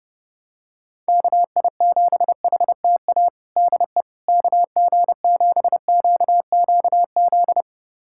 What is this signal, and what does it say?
Morse code, heard as a keyed tone, about 30 wpm, 700 Hz. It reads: KI7HTA DE KG7QQZ
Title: USA Callsign DE Callsign - 6 Characters - 30wpm